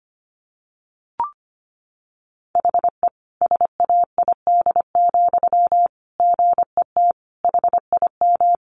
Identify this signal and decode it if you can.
Morse code, heard as a keyed tone, about 25 wpm, 700 Hz. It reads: HE SAID, GET HIM
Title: Sentences from Top 400 Words 25wpm